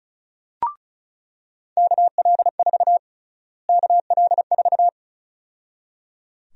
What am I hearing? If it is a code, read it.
Morse code, heard as a keyed tone, about 34 wpm, 700 Hz. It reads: KL4 KL4
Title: USA Call Sign Prefixes - Repeated 2x - 35wpm